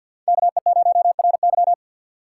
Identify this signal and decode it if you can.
Morse code, heard as a keyed tone, about 63 wpm, 700 Hz. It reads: KE0RY